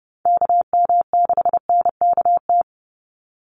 Morse code, heard as a keyed tone, about 30 wpm, 700 Hz. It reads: KM6NKT
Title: USA Call Signs 30wpm